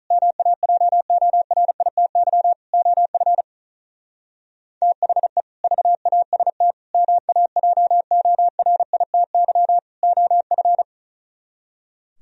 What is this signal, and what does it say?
Morse code, heard as a keyed tone, about 44 wpm, 700 Hz. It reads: MAJORITY OF THE VAST MAJORITY OF
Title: Speed-Racing - Sets of 4 Words 35wpm